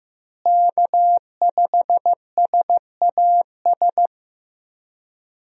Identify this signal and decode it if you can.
Morse code, heard as a keyed tone, about 15 wpm, 700 Hz. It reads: K5SAS
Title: USA Call Signs - 5 Characters - Repeated 3x - 15wpm